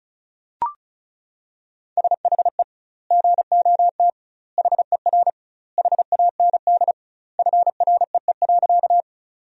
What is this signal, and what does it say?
Morse code, heard as a keyed tone, about 34 wpm, 700 Hz. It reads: SHE GOT HER HAND FREE.
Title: 2x Word-Spacing - Sentences from Top 500 Words - 35wpm